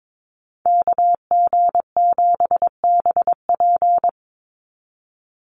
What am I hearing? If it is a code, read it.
Morse code, heard as a keyed tone, about 22 wpm, 700 Hz. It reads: KG7BP
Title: USA Call Signs 22wpm